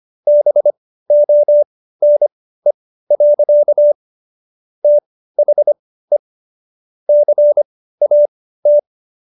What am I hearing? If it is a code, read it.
Morse code, heard as a keyed tone, about 25 wpm, spaced as for 15 wpm, 590 Hz. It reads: BONE. THE CAT